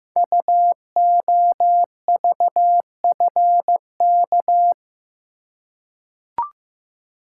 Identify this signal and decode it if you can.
Morse code, heard as a keyed tone, about 15 wpm, 700 Hz. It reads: UOVFK
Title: Code Groups - 5 Letters - 15wpm